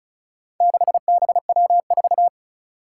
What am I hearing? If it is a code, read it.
Morse code, heard as a keyed tone, about 36 wpm, 700 Hz. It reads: 6BW4
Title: Four Challenging Letter-Number Combinations 35wpm